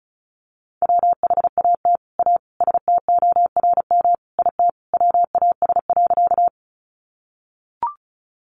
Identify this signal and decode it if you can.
Morse code, heard as a keyed tone, about 34 wpm, 700 Hz. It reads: WHAT A STORM IT WAS.
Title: Sentences from Top 1500 Words 35wpm